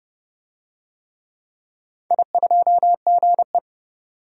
Morse code, heard as a keyed tone, about 30 wpm, 700 Hz. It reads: I2GE
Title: DXCC-100 Call Signs - 4 Characters - 30wpm